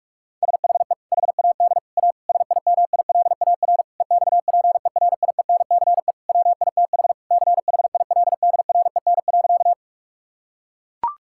Morse code, heard as a keyed tone, about 41 wpm, 700 Hz. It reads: SHE HAD A SIMILAR EXPERIENCE WITH CHILDREN.
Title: Sentences from Top 1000 Words - Review Part 5 - 45wpm